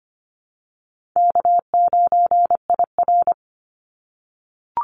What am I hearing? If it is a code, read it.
Morse code, heard as a keyed tone, about 25 wpm, 700 Hz. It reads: K9IR
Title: Call Signs - 25wpm